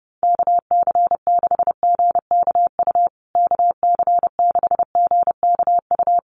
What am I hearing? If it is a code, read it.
Morse code, heard as a keyed tone, about 30 wpm, 700 Hz. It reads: KC6GKU KC6GKU
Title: USA Call Signs Repeated 3x 30wpm